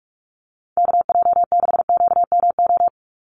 Morse code, heard as a keyed tone, about 44 wpm, 700 Hz. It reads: KJ6QMO
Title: USA Call Signs 45wpm